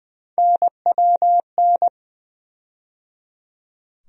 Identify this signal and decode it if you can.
Morse code, heard as a keyed tone, about 20 wpm, 700 Hz. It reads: NWN
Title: USA Call Sign Postfixes - 3 Letters 20wpm